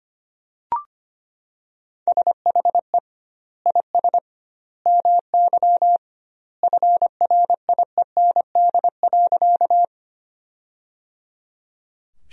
Morse code, heard as a keyed tone, about 25 wpm, 700 Hz. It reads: SHE IS MY FRIEND.
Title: 2x Word-Spacing - Sentences from Top 300 Words - 25wpm